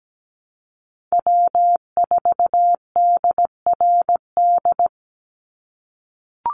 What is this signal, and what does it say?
Morse code, heard as a keyed tone, about 17 wpm, 700 Hz. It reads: W4DRD
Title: USA Call Signs 17wpm